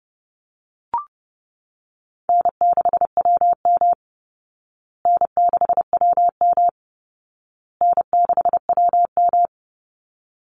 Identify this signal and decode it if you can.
Morse code, heard as a keyed tone, about 30 wpm, 700 Hz. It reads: N6WM N6WM N6WM
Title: Easy Call Signs Repeated 3x - 30wpm